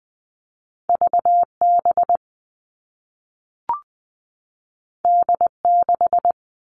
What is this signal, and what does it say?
Morse code, heard as a keyed tone, about 20 wpm, 700 Hz. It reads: V B D 6
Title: All Cut Numbers - 2-Digit 20wpm